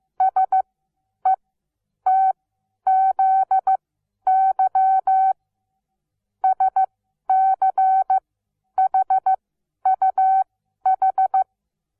Morse code, heard as a keyed tone, about 15 wpm, spaced as for 10 wpm, 760 Hz. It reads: SETZY SCHUH
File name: Onlinelehrgang / morsen / DJ4UF Morsen17